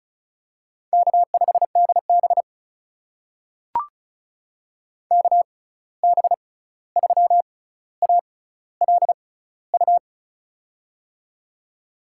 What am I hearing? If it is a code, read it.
Morse code, heard as a keyed tone, about 36 wpm, spaced as for 16 wpm, 700 Hz. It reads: K5DB KB3ALU
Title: USA Call Signs - 6x Character Spacing - 35wpm